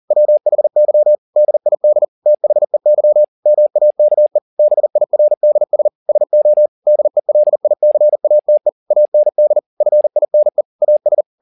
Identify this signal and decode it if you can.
Morse code, heard as a keyed tone, about 40 wpm, 600 Hz. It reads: WHY DID THEY MAKE BIRDS SO DELICATE AND FINE AS